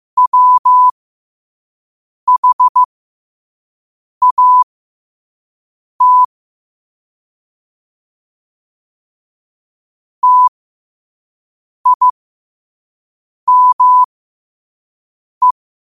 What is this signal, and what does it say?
Morse code, heard as a keyed tone, about 15 wpm, spaced as for 5 wpm, 1000 Hz. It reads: WHAT TIME
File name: qotdfiles / 5wpm